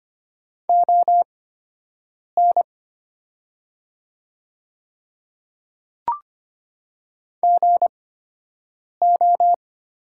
Morse code, heard as a keyed tone, about 25 wpm, spaced as for 7 wpm, 700 Hz. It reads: ON GO
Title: 8x Character Spacing - Top 100 Words - 25wpm